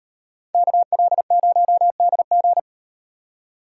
Morse code, heard as a keyed tone, about 37 wpm, 700 Hz. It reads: KL0DG